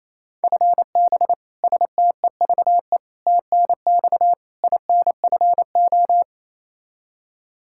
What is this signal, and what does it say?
Morse code, heard as a keyed tone, about 28 wpm, 700 Hz. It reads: FB STEVE TNX INFO
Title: Practice QSO - Exchange 3 - More Info - 28wpm